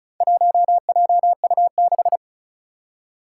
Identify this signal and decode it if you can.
Morse code, heard as a keyed tone, about 36 wpm, 700 Hz. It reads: 1JU6
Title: Four Challenging Letter-Number Combinations 35wpm